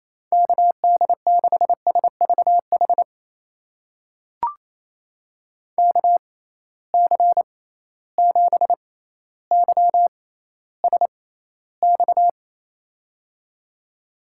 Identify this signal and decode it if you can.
Morse code, heard as a keyed tone, about 28 wpm, spaced as for 14 wpm, 700 Hz. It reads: KD6SVH KC7YSX
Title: USA Call Signs - 6 Characters - 6x Character Spacing - 28wpm